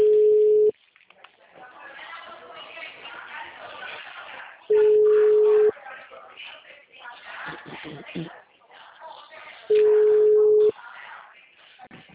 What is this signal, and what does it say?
Ringback tone (Europe, 425 Hz), with speech.
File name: out- -2010- - - .5294